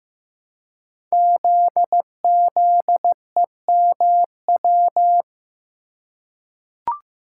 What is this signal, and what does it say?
Morse code, heard as a keyed tone, about 15 wpm, 700 Hz. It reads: ZZEMW
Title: Code Groups - 5 Letters - 15wpm